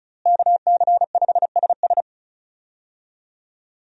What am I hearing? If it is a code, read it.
Morse code, heard as a keyed tone, about 34 wpm, 700 Hz. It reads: KC5SS
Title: USA Call Signs - 5 Characters - Repeated 2x - 35wpm